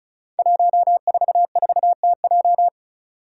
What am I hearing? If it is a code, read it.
Morse code, heard as a keyed tone, about 36 wpm, 700 Hz. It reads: 144TJ
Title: Five Challenging Letter-Number Combinations 35wpm